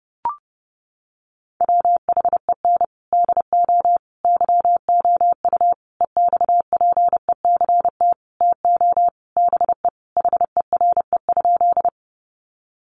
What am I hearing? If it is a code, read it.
Morse code, heard as a keyed tone, about 30 wpm, 700 Hz. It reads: WHEN DO YOU EXPECT TO BE HERE?
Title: Sentences from Top 1100 Words 30wpm